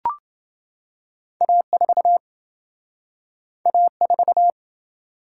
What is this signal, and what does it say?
Morse code, heard as a keyed tone, about 27 wpm, 700 Hz. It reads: A4 A4